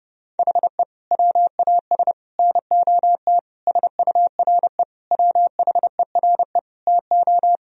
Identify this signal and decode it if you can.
Morse code, heard as a keyed tone, about 30 wpm, 700 Hz. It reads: HE WAS NOT SURE WHERE TO
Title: Sentences from Top 1000 Words - Review Part 1 - 30wpm